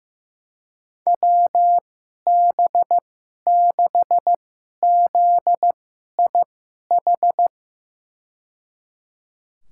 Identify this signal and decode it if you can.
Morse code, heard as a keyed tone, about 16 wpm, spaced as for 11 wpm, 700 Hz. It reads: WB6ZIH
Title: USA Call Signs - 6 Characters - 2x Character Spacing - 15wpm